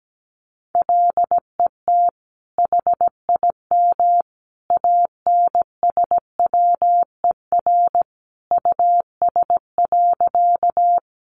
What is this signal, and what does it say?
Morse code, heard as a keyed tone, about 17 wpm, 700 Hz. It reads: LET HIM ANSWER US.
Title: Sentences from Top 300 Words 17wpm